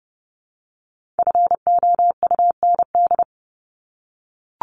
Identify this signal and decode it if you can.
Morse code, heard as a keyed tone, about 30 wpm, 700 Hz. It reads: FOUND
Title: Top 300 Words Repeated 30wpm